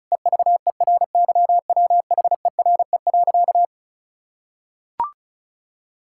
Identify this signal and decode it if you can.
Morse code, heard as a keyed tone, about 36 wpm, 700 Hz. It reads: EVERYWHERE.
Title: Sentences from Top 1600 Words 35wpm